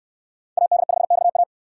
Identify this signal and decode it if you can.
Morse code, heard as a keyed tone, about 69 wpm, 700 Hz. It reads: AD5XA